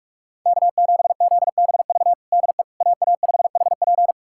Morse code, heard as a keyed tone, about 47 wpm, 700 Hz. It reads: K7ZBV DE AA5HP
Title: USA Callsign DE Callsign - 5 Characters - 45wpm